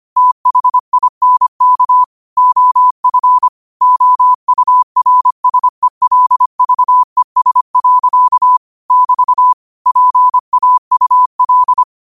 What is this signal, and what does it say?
Morse code, heard as a keyed tone, about 25 wpm, 1000 Hz. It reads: THINK OF OURSELVES. <BT> PAUL